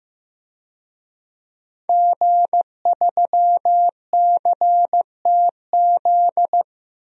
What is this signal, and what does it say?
Morse code, heard as a keyed tone, about 15 wpm, 700 Hz. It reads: G3CTZ
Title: DXCC-100 Call Signs - 5 Characters - 15wpm